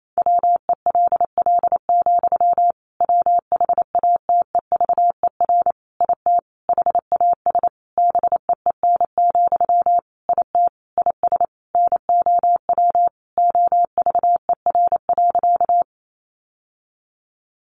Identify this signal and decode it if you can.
Morse code, heard as a keyed tone, about 28 wpm, 700 Hz. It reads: WELL, WHATEVER IT HAS BEEN, IT IS NOW OVER.
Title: Sentences from Top 1100 Words 28wpm